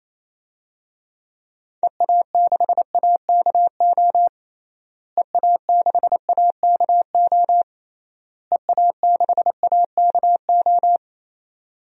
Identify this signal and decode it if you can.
Morse code, heard as a keyed tone, about 28 wpm, 700 Hz. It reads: EA6AKO EA6AKO EA6AKO
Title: DXCC-100 Call Signs - 6 Characters - Repeated 3x - 28wpm